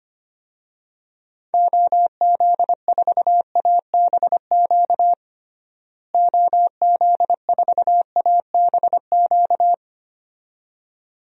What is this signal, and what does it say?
Morse code, heard as a keyed tone, about 25 wpm, 700 Hz. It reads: OZ4ABQ OZ4ABQ
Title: DXCC-100 Call Signs - 6 Characters - Repeated 2x - 25wpm